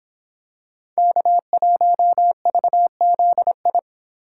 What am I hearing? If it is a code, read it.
Morse code, heard as a keyed tone, about 26 wpm, 700 Hz. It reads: K1VZI